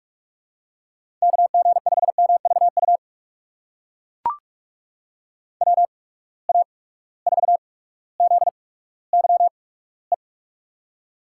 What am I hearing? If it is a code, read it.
Morse code, heard as a keyed tone, about 44 wpm, spaced as for 19 wpm, 700 Hz. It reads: KG5MVU WA4ZYE
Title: USA Call Signs - 6 Characters - 8x Character Spacing - 45wpm